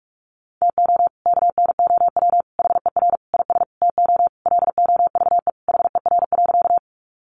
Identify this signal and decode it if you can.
Morse code, heard as a keyed tone, about 44 wpm, 700 Hz. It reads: TO KNOW HER IS TO LOVE HER.
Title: Sentences from Top 400 Words 45wpm